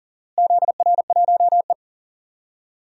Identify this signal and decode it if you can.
Morse code, heard as a keyed tone, about 37 wpm, 700 Hz. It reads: ZR1E
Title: DXCC-100 Call Signs - 4 Characters - Repeated 2x - 40wpm